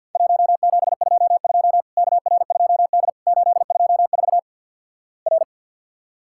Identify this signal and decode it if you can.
Morse code, heard as a keyed tone, about 52 wpm, 700 Hz. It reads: 1722 KR2D 824 R
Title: Worked All Europe DX Contest - QTC Exchange - 50wpm